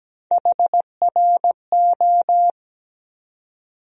Morse code, heard as a keyed tone, about 17 wpm, 700 Hz. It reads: HRO
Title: USA Call Sign Postfixes - 3 Letters 17wpm